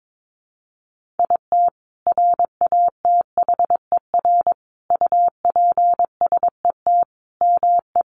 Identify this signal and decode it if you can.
Morse code, heard as a keyed tone, about 22 wpm, 700 Hz. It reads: IT RATHER UPSET ME
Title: Sentences from Top 900 Words 22wpm